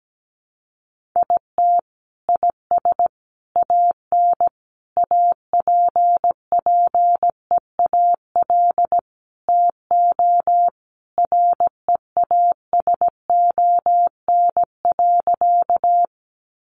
Morse code, heard as a keyed tone, about 17 wpm, 700 Hz. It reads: IT IS AN APPEAL TO REASON.
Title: Sentences from Top 1300 Words 17wpm